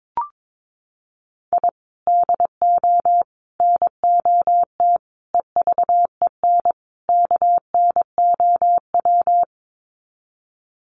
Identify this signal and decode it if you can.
Morse code, heard as a keyed tone, about 22 wpm, 700 Hz. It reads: I DO NOT EVEN KNOW
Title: Sets of 5 Words 22wpm